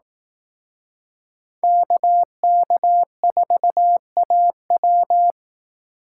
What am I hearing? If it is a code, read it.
Morse code, heard as a keyed tone, about 18 wpm, 700 Hz. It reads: KK4AW
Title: Speed Racing - USA Call Signs 15wpm